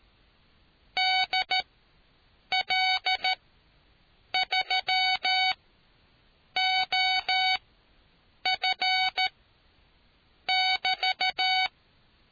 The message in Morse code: DL3OF<BT>

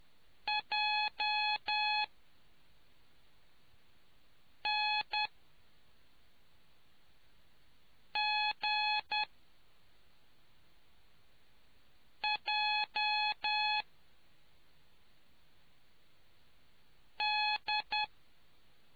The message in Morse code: JNGJD

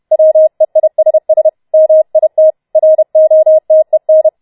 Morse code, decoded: WEISS MIT ROTEN

